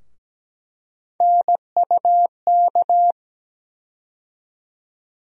Morse code, heard as a keyed tone, about 17 wpm, 700 Hz. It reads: NUK
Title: USA Call Sign Postfixes - 3 Letters 17wpm